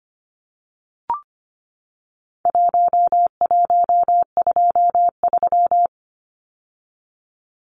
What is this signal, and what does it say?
Morse code, heard as a keyed tone, about 25 wpm, 700 Hz. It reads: 1123
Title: Numbers - 4-Digit 25wpm